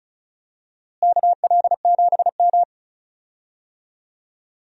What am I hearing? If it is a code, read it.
Morse code, heard as a keyed tone, about 34 wpm, 700 Hz. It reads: KL7M
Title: DXCC-100 Call Signs - 4 Characters - Repeated 3x - 35wpm